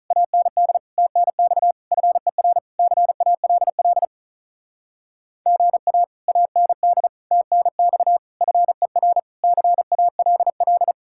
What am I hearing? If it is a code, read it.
Morse code, heard as a keyed tone, about 48 wpm, 700 Hz. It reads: AND TNX FER CALL GA AND TNX FER CALL